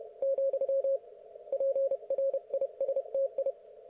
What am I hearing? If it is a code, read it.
Morse code, heard as a keyed tone, about 31 wpm, 530 Hz. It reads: , PRISTI
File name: Archiv / CUC107 / ok1fhi cul107